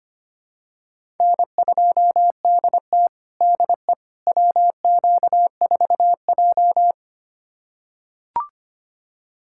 Morse code, heard as a keyed tone, about 25 wpm, 700 Hz. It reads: N2DT DE WQ4J
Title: USA Callsign DE Callsign - 4 Characters - 25wpm